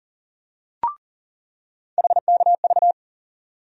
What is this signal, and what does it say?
Morse code, heard as a keyed tone, about 38 wpm, 700 Hz. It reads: HKV
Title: USA Call Sign Postfixes - 3 Letters 40wpm